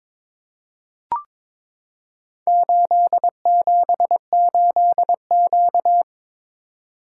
Morse code, heard as a keyed tone, about 22 wpm, 700 Hz. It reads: 878Q
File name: practice / 22wpm / Four Letter-Number Combinations - 22wpm